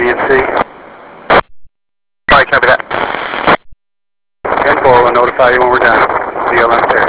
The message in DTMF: none